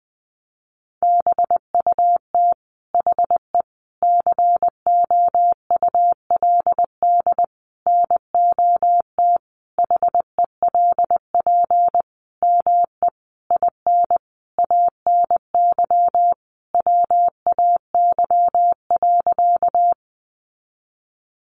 Morse code, heard as a keyed tone, about 20 wpm, 700 Hz. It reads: BUT HE COULD NOT HELP ME IN ANY WAY.
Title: Sentences from Top 600 Words 20wpm